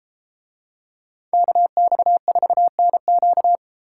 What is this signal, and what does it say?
Morse code, heard as a keyed tone, about 32 wpm, 700 Hz. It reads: KX4NQ